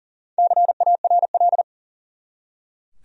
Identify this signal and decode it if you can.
Morse code, heard as a keyed tone, about 40 wpm, 700 Hz. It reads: CARL
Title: USA Common Names - Max 5 Letters - 40wpm